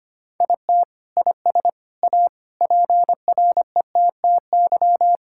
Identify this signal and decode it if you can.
Morse code, heard as a keyed tone, about 25 wpm, 700 Hz. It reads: IT IS A PRETTY
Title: Sentences from Top 1000 Words - Review Part 6 - 25wpm